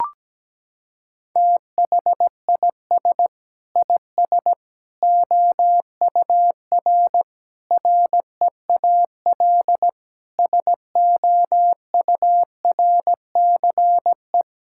Morse code, heard as a keyed tone, about 17 wpm, 700 Hz. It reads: THIS IS OUR REAL SOURCE
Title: Sentences from Top 1000 Words - Review Part 5 - 17wpm